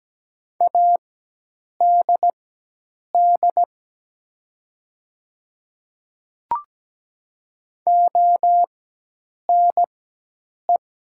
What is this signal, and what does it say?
Morse code, heard as a keyed tone, about 17 wpm, spaced as for 8 wpm, 700 Hz. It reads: ADD ONE
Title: 4x Character Spacing - Top 500 Words - 17wpm